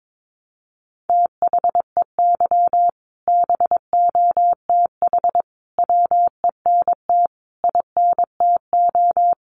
Morse code, heard as a keyed tone, about 22 wpm, 700 Hz. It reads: THEY BOTH WENT INTO
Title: Sentences from Top 600 Words 22wpm